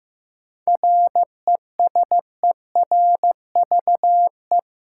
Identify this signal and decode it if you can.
Morse code, heard as a keyed tone, about 15 wpm, 700 Hz. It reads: RESERVE
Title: Sentences from Top 1000 Words - Review Part 6 - New Word Drill - 15wpm